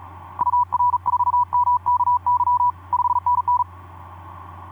Morse code, heard as a keyed tone, about 40 wpm, 980 Hz. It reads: UR4MKY 5NN
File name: d44ac ek2rgt etc 10m cw 23nov2010 1202